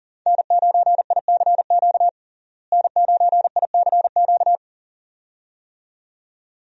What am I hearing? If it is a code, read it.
Morse code, heard as a keyed tone, about 40 wpm, 700 Hz. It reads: N9ICQ N9ICQ
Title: USA Call Signs - 5 Characters - Repeated 2x - 40wpm